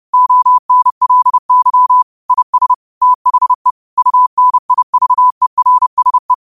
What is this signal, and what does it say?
Morse code, heard as a keyed tone, about 30 wpm, 1000 Hz. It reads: ONLY IS THE UNIVERSE
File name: qotdfiles / 30wpm